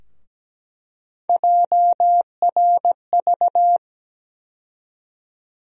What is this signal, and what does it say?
Morse code, heard as a keyed tone, about 17 wpm, 700 Hz. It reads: JRV
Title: USA Call Sign Postfixes - 3 Letters 17wpm